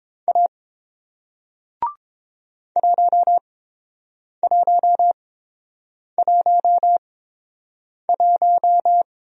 Morse code, heard as a keyed tone, about 30 wpm, spaced as for 7 wpm, 700 Hz. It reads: A 1111